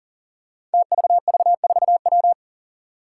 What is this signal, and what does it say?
Morse code, heard as a keyed tone, about 43 wpm, 700 Hz. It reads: TVV4W
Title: Five Challenging Letter-Number Combinations 40wpm